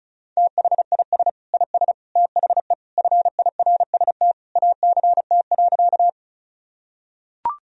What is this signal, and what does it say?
Morse code, heard as a keyed tone, about 34 wpm, 700 Hz. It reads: THIS IS THE FIRST ACT.
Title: Sentences from Top 1000 Words - Review Part 1 - 35wpm